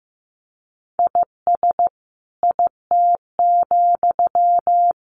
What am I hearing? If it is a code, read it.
Morse code, heard as a keyed tone, about 15 wpm, 700 Hz. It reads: IS IT,
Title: Sentences from Top 1500 Words 15wpm